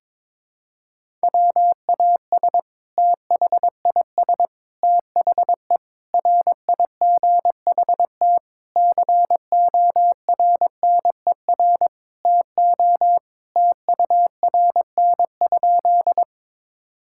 Morse code, heard as a keyed tone, about 22 wpm, 700 Hz. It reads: WAS THIS THE RIGHT CORNER TO TURN?
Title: Sentences from Top 1600 Words 22wpm